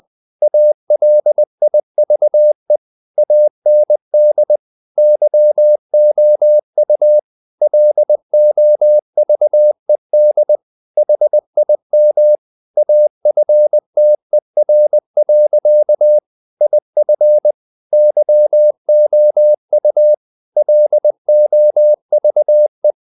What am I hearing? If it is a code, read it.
Morse code, heard as a keyed tone, about 20 wpm, 600 Hz. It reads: ALIVE AND YOU LOVED HIM AFTER. IF YOU LOVE
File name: the old man and the sea part 2 20wpm